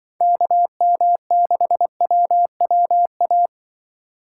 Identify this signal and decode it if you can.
Morse code, heard as a keyed tone, about 24 wpm, 700 Hz. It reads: KM6WWA
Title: Speed Racing - USA 2x3 Call Signs 20wpm